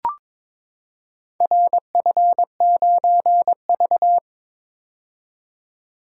Morse code, heard as a keyed tone, about 21 wpm, 700 Hz. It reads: RF9V
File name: practice / 22wpm / Four Letter-Number Combinations - 22wpm